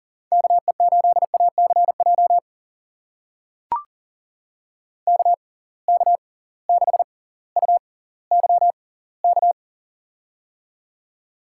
Morse code, heard as a keyed tone, about 40 wpm, spaced as for 20 wpm, 700 Hz. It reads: KE8ACJ KK6UYK
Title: USA Call Signs - 6x Character Spacing - 40wpm